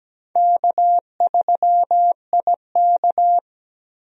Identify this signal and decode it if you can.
Morse code, heard as a keyed tone, about 17 wpm, 700 Hz. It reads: K3IK
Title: USA Call Signs - 4 Characters - 17wpm